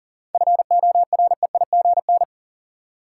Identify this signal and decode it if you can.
Morse code, heard as a keyed tone, about 40 wpm, 700 Hz. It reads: FOREIGN
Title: Sentences from Top 1400 Words - New Word Drill Alt - 40wpm